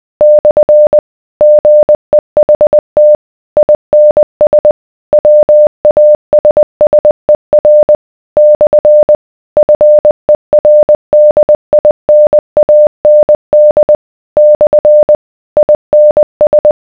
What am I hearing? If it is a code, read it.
Morse code, heard as a keyed tone, about 20 wpm, 600 Hz. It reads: / GEHT INS WASSER / FERDINAND / INS